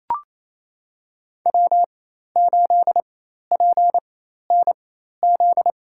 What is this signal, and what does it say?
Morse code, heard as a keyed tone, about 28 wpm, spaced as for 13 wpm, 700 Hz. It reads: W8PNZ
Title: USA Call Signs - 5 Characters - 4x Character Spacing - 28wpm